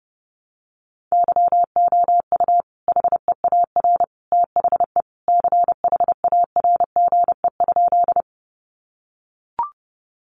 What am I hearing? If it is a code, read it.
Morse code, heard as a keyed tone, about 30 wpm, 700 Hz. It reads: YOU HEAR THE CHARGE?
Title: Sentences from Top 900 Words 30wpm